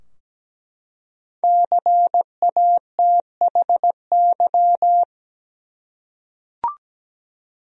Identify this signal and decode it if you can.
Morse code, heard as a keyed tone, about 17 wpm, 700 Hz. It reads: CATHY
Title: USA Common Names - Max 5 Letters - 17wpm